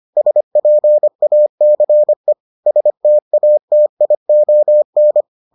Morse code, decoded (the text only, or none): SPACE STATION